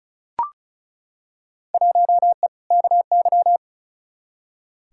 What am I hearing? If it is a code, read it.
Morse code, heard as a keyed tone, about 34 wpm, 700 Hz. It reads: 1E KY
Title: Field Day Exchanges - 35wpm